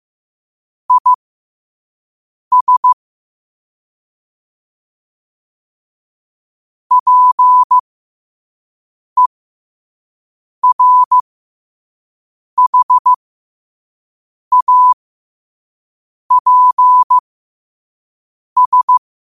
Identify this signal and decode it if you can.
Morse code, heard as a keyed tone, about 15 wpm, spaced as for 5 wpm, 1000 Hz. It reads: IS PERHAPS
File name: qotdfiles / 5wpm